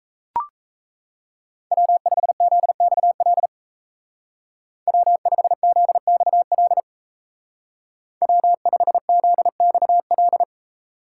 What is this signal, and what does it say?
Morse code, heard as a keyed tone, about 35 wpm, 700 Hz. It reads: W5ZXL W5ZXL W5ZXL
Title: Speed Racing - USA 1x3 Call Signs 28wpm